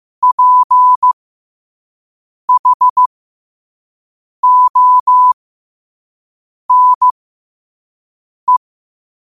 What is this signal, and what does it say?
Morse code, heard as a keyed tone, about 15 wpm, spaced as for 5 wpm, 1000 Hz. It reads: PHONE
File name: qotdfiles / 5wpm